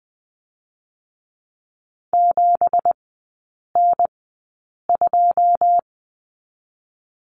Morse code, heard as a keyed tone, about 20 wpm, 700 Hz. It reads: 7 N 2
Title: 205 Lesson 33e - Letter j - Three Characters 20wpm